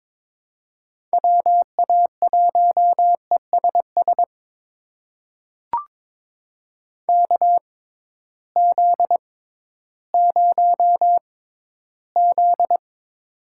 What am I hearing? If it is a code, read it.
Morse code, heard as a keyed tone, about 22 wpm, spaced as for 13 wpm, 700 Hz. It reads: WA1ESS KZ0Z
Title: USA Call Signs - 6x Character Spacing - 22wpm